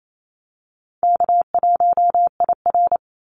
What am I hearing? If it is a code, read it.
Morse code, heard as a keyed tone, about 28 wpm, 700 Hz. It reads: K1IR
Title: Call Signs - 28wpm